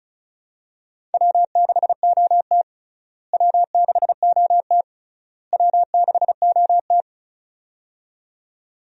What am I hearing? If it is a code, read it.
Morse code, heard as a keyed tone, about 36 wpm, 700 Hz. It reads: W6OT W6OT W6OT
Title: USA Call Signs - 4 Characters - Repeated 3x - 35wpm